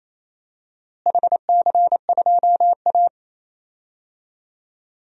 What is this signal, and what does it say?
Morse code, heard as a keyed tone, about 28 wpm, 700 Hz. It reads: HC2A
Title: DXCC-100 Call Signs - 4 Characters - Repeated 3x - 28wpm